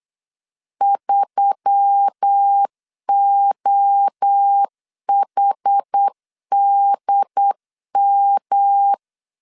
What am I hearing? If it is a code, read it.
Morse code, heard as a keyed tone, about 8 wpm, 800 Hz. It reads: 3OHDM
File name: 8wpm